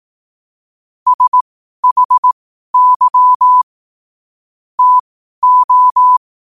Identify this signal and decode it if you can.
Morse code, heard as a keyed tone, about 18 wpm, spaced as for 12 wpm, 1000 Hz. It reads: SHY TO
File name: qotdfiles / 12wpm